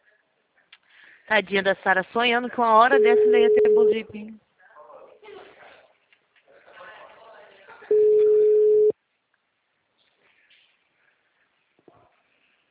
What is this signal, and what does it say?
Ringback tone (Europe, 425 Hz), with speech.